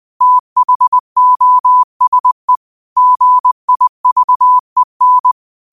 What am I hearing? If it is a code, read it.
Morse code, heard as a keyed tone, about 20 wpm, 1000 Hz. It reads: THOSE GIVEN
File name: qotdfiles / 20wpm